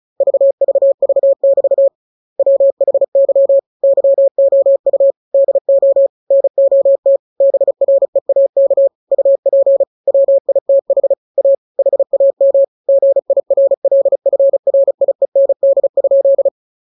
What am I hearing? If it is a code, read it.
Morse code, heard as a keyed tone, about 35 wpm, 550 Hz. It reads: VVV<BT> WHY YOU DO NOT BREAK UP WITH A HAM GIRLFRIEND?
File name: jokes / joke 18 35 0000